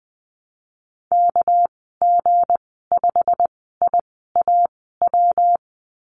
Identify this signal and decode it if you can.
Morse code, heard as a keyed tone, about 20 wpm, spaced as for 15 wpm, 700 Hz. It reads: KG5IAW
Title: 238 Lesson 38g - Letter q - Call Signs 20wpm